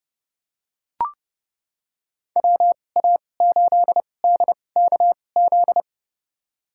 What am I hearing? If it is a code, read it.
Morse code, heard as a keyed tone, about 30 wpm, spaced as for 22 wpm, 700 Hz. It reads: WA8DKZ
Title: USA Call Signs - 6 Characters - 2x Character Spacing - 30wpm